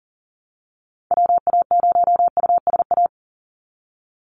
Morse code, heard as a keyed tone, about 40 wpm, 700 Hz. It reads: WA0USA
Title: Call Signs Repeated 3x - 40wpm